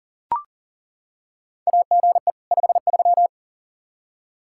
Morse code, heard as a keyed tone, about 40 wpm, 700 Hz. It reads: AGE 53
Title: Practice QSO - Exchange 3 - More Info - 40wpm